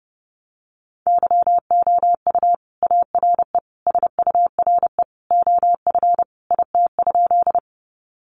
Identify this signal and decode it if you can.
Morse code, heard as a keyed tone, about 30 wpm, 700 Hz. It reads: YOU ARE SURE OF IT?
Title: Sentences from Top 300 Words 30wpm